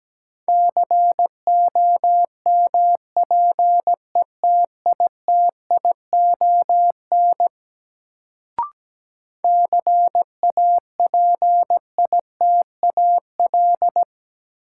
Morse code, heard as a keyed tone, about 17 wpm, 700 Hz. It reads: COMPETITION CAPITAL
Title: Sentences from Top 1000 Words - New Word Drill - 17wpm